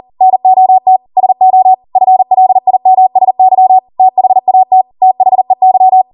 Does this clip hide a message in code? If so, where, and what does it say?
Yes: Morse code from 0.2 to 6.0 s, heard as a keyed tone, about 40 wpm, 750 Hz. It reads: NOT SO FLIMSY THAT THEY